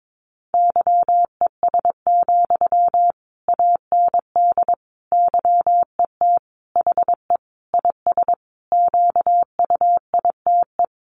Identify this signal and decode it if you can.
Morse code, heard as a keyed tone, about 22 wpm, 700 Hz. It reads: YES, AND YET HE IS QUITE
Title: Sentences from Top 700 Words 22wpm